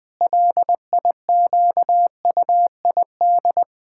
Morse code, heard as a keyed tone, about 20 wpm, 700 Hz. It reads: LIQUID
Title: Pangram Challenge v2 - 20wpm